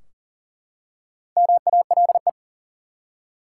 Morse code, heard as a keyed tone, about 38 wpm, 700 Hz. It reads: MALE
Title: Four Letter Words - 40wpm